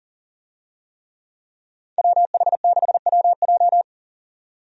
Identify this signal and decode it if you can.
Morse code, heard as a keyed tone, about 40 wpm, 700 Hz. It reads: WH6WJ
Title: USA Call Signs - 5 Characters - 40wpm